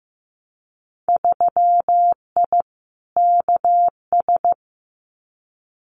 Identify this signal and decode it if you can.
Morse code, heard as a keyed tone, about 15 wpm, 700 Hz. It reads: 3I KS
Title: Winter Field Day Exchanges Encore 15wpm